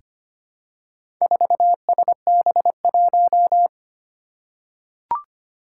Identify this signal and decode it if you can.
Morse code, heard as a keyed tone, about 25 wpm, 700 Hz. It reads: 4SB1
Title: Four Challenging Letter-Number Combinations 25wpm